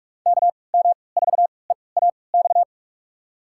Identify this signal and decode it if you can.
Morse code, heard as a keyed tone, about 46 wpm, spaced as for 26 wpm, 700 Hz. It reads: KM4EAX
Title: USA Call Signs - 6 Characters - 3x Character Spacing - 45wpm